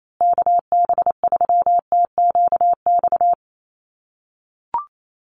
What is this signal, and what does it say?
Morse code, heard as a keyed tone, about 28 wpm, 700 Hz. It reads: KB3TQX
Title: USA Call Signs Repeated 2x 28wpm